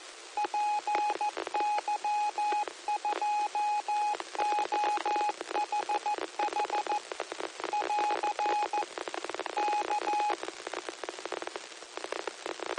Morse code, heard as a keyed tone, about 14 wpm, 840 Hz, with radio noise. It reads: PY2OHH <AR> K